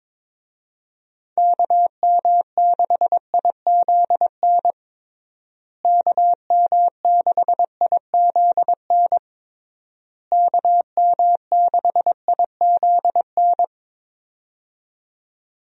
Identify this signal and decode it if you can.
Morse code, heard as a keyed tone, about 22 wpm, 700 Hz. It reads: KM6IZN KM6IZN KM6IZN
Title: USA Call Signs - 6 Characters - Repeated 3x - 22wpm